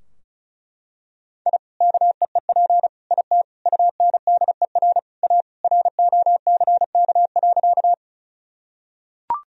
Morse code, heard as a keyed tone, about 34 wpm, 700 Hz. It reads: I KEEP IT UNDER A ROCK.
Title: Sentences from Top 1000 Words - Review Part 2 - 35wpm